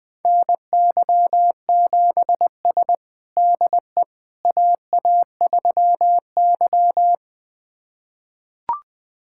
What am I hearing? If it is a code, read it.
Morse code, heard as a keyed tone, about 20 wpm, 700 Hz. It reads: NY7S DE AA3Y
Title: USA Callsign DE Callsign - 4 Characters - 20wpm